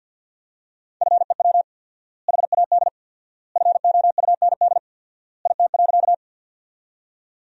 Morse code, heard as a keyed tone, about 53 wpm, 700 Hz. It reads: FEW HAD FOUND IT.
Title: 4x Word-Spacing - Sentences from Top 300 Words - 50wpm